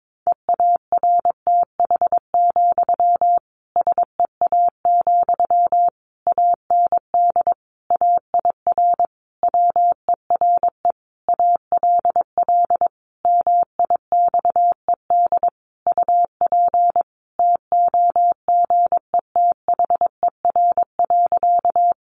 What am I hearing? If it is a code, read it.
Morse code, heard as a keyed tone, about 22 wpm, 700 Hz. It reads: EARTH, SEA, AND AIR WERE ALL MIXED UP TOGETHER.
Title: Sentences from Top 1500 Words 22wpm